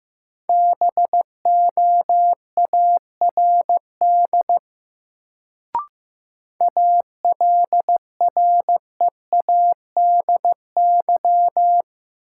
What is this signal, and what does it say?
Morse code, heard as a keyed tone, about 15 wpm, 700 Hz. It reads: BOARD ALREADY
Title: Sentences from Top 700 Words - New Word Drill - 15wpm